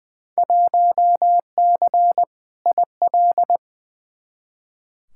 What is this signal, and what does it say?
Morse code, heard as a keyed tone, about 20 wpm, 700 Hz. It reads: 1C IL
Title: Field Day Exchanges - 20wpm